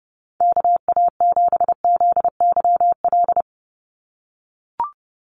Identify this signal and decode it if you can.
Morse code, heard as a keyed tone, about 30 wpm, 700 Hz. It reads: KA7ZYL
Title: USA Call Signs 30wpm